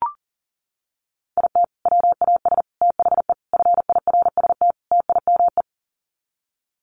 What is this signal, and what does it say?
Morse code, heard as a keyed tone, about 40 wpm, 700 Hz. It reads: IT WAS THE FIRST TIME
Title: Sets of 5 Words 40wpm